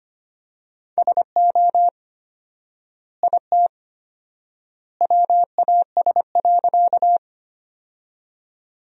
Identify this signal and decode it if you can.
Morse code, heard as a keyed tone, about 25 wpm, 700 Hz. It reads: SO IT WAS.
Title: 4x Word-Spacing - Sentences from Top 100 Words - 25wpm